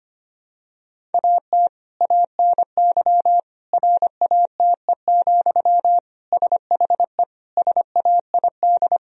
Morse code, heard as a keyed tone, about 25 wpm, 700 Hz. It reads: AT ANY RATE, SHE SAID
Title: Sentences from Top 1000 Words - Review Part 4 - 25wpm